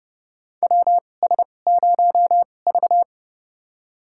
Morse code, heard as a keyed tone, about 30 wpm, spaced as for 22 wpm, 690 Hz. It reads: WS0V
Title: USA Call Signs - 4 Characters - 2x Character Spacing - 30wpm